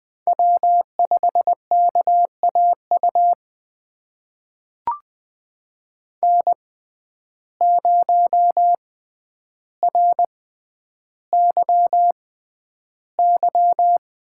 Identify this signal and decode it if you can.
Morse code, heard as a keyed tone, about 20 wpm, spaced as for 10 wpm, 700 Hz. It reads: W5KAU N0RYY
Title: USA Call Signs - 5 Characters - 6x Character Spacing - 20wpm